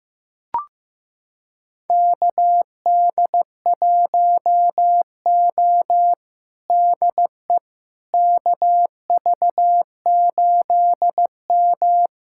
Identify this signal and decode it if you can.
Morse code, heard as a keyed tone, about 15 wpm, 700 Hz. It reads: KD1O DE KV8M
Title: USA Callsign DE Callsign - 4 Characters - 15wpm